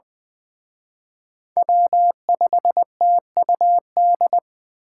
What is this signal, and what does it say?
Morse code, heard as a keyed tone, about 23 wpm, 700 Hz. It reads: W5TUD
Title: Speed Racing - USA 1x3 Call Signs 20wpm